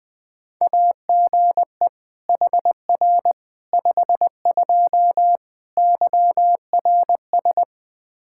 Morse code, heard as a keyed tone, about 21 wpm, 700 Hz. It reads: AGE HR 52 YRS
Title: Practice QSO - Exchange 3 - More Info - 20wpm